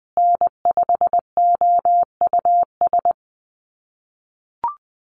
Morse code, heard as a keyed tone, about 20 wpm, 700 Hz. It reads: N5OUS
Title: USA Call Signs Repeated 2x 20wpm